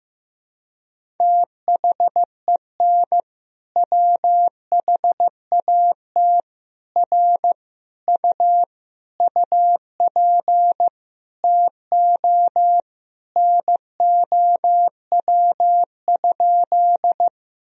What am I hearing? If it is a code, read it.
Morse code, heard as a keyed tone, about 16 wpm, 700 Hz. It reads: THEN WHAT R U UP TO NOW?
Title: Sentences from Top 100 Words and Abbreviations 15wpm